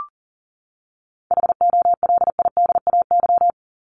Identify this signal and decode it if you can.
Morse code, heard as a keyed tone, about 40 wpm, 700 Hz. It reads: HOLIDAY
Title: Top 1501-1750 Words 40wpm